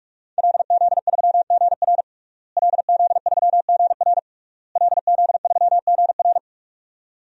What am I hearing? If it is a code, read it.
Morse code, heard as a keyed tone, about 41 wpm, 700 Hz. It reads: LZ3GR LZ3GR LZ3GR
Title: DXCC-100 Call Signs - 5 Characters - Repeated 3x - 45wpm